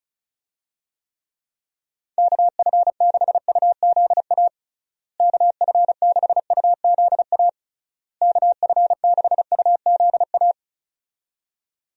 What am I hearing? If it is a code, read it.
Morse code, heard as a keyed tone, about 36 wpm, 700 Hz. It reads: KF6UZA KF6UZA KF6UZA
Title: USA Call Signs - 6 Characters - Repeated 3x - 35wpm